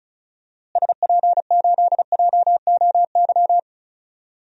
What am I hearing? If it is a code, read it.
Morse code, heard as a keyed tone, about 35 wpm, 700 Hz. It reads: SP8JOY
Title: DXCC-100 Call Signs - 6 Characters - Repeated 2x - 35wpm